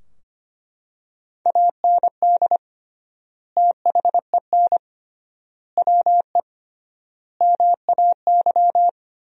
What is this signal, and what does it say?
Morse code, heard as a keyed tone, about 25 wpm, 700 Hz. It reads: AND THEN WE MAY
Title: 3x Word-Spacing - Sentences from Top 300 Words - 25wpm